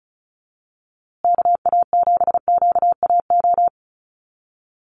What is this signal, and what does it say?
Morse code, heard as a keyed tone, about 34 wpm, 700 Hz. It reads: KA7QAO
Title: USA Call Signs Repeated 2x 35wpm